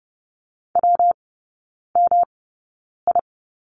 Morse code, heard as a keyed tone, about 30 wpm, spaced as for 9 wpm, 700 Hz. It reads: WMI
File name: Lesson 19e - Letter w - Three Characters 30wpm